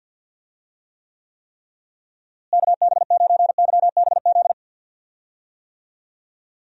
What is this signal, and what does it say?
Morse code, heard as a keyed tone, about 52 wpm, 700 Hz. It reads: KB9YBZ
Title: USA Call Signs - 6 Characters - Repeated 3x - 50wpm